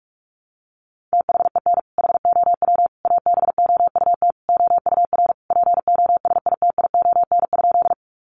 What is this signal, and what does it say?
Morse code, heard as a keyed tone, about 44 wpm, 700 Hz. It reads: THEN HOW ABOUT OUR POSITION?
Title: Sentences from Top 900 Words 45wpm